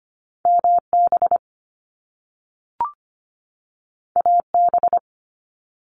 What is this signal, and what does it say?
Morse code, heard as a keyed tone, about 25 wpm, 700 Hz. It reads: MB AB
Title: Canadian Provinces and Territories 25wpm